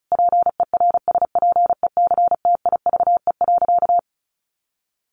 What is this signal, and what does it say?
Morse code, heard as a keyed tone, about 34 wpm, 700 Hz. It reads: PERSPECTIVE.
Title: Sentences from Top 1200 Words 35wpm